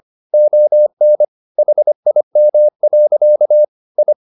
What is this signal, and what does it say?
Morse code, heard as a keyed tone, about 25 wpm, 600 Hz. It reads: ON HIM. I